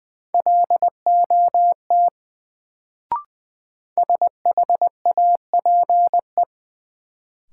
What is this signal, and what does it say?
Morse code, heard as a keyed tone, about 20 wpm, 700 Hz. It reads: LOT SHAPE
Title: Sentences from Top 500 Words - New Word Drill - 20wpm